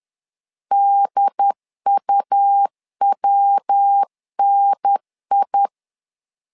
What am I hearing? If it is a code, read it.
Morse code, heard as a keyed tone, about 10 wpm, 800 Hz. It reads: DUWNI
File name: LEZIONI / 3 MWRO / 3 MW 10wpm 1959